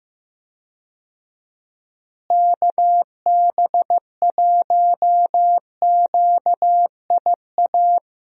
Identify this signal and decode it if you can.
Morse code, heard as a keyed tone, about 16 wpm, 700 Hz. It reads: KB1QIA
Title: USA Call Signs - 6 Characters - Repeated 3x - 15wpm